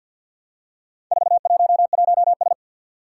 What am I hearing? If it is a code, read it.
Morse code, heard as a keyed tone, about 48 wpm, 700 Hz. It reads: 411S
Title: Four Challenging Letter-Number Combinations 50wpm